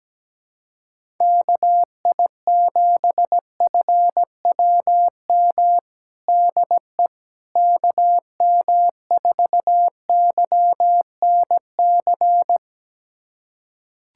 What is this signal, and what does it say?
Morse code, heard as a keyed tone, about 17 wpm, 700 Hz. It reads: KI7FWM DE KM4YNC
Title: USA Callsign DE Callsign - 6 Characters - 17wpm